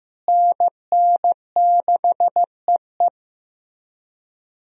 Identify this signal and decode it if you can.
Morse code, heard as a keyed tone, about 15 wpm, 700 Hz. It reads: NN6EE
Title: USA Call Signs - 5 Characters - Repeated 2x - 15wpm